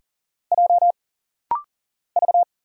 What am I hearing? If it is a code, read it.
Morse code, heard as a keyed tone, about 40 wpm, 700 Hz. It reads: J V
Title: Single Letters - Rapid-Fire 40wpm